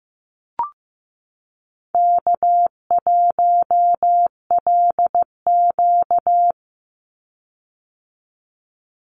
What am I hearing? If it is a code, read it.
Morse code, heard as a keyed tone, about 15 wpm, 700 Hz. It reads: K1LQ
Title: USA Call Signs 15wpm